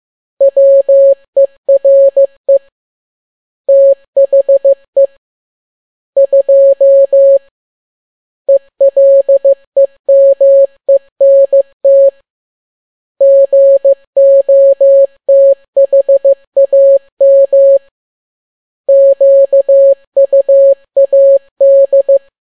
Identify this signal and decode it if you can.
Morse code, heard as a keyed tone, about 15 wpm, 550 Hz. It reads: WERE THE 2 ELEMENT GOTHAM QUAD
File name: n6tt p1 15C13